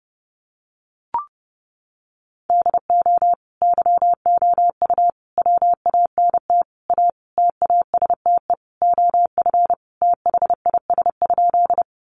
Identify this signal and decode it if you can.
Morse code, heard as a keyed tone, about 30 wpm, 700 Hz. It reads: DO YOU WANT A TASTE OF THIS?
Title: Sentences from Top 1100 Words 30wpm